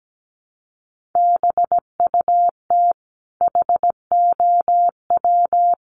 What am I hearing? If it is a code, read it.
Morse code, heard as a keyed tone, about 17 wpm, 700 Hz. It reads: BUT HOW
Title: Sentences from Top 1400 Words 17wpm